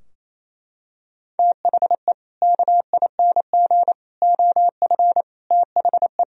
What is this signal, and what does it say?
Morse code, heard as a keyed tone, about 27 wpm, 700 Hz. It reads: THE KING OF THE
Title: Sentences from Top 1000 Words - Review Part 2 - 28wpm